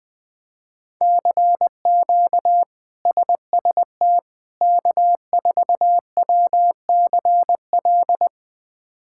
Morse code, heard as a keyed tone, about 21 wpm, 700 Hz. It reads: CQ SST K4WCL
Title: K1USN SST QSO Practice - 20wpm